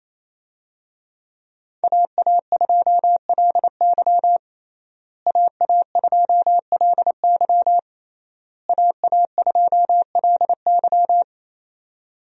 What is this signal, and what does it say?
Morse code, heard as a keyed tone, about 28 wpm, 700 Hz. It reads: AA2LY AA2LY AA2LY
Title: USA Call Signs - 5 Characters - Repeated 3x - 28wpm